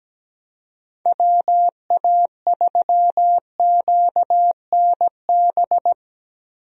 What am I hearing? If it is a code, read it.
Morse code, heard as a keyed tone, about 17 wpm, 700 Hz. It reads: WA3QNB
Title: USA Call Signs - 6 Characters - Repeated 2x - 17wpm